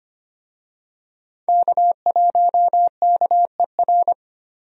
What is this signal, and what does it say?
Morse code, heard as a keyed tone, about 25 wpm, 700 Hz. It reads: K1KER
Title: USA Call Signs - 5 Characters - 25wpm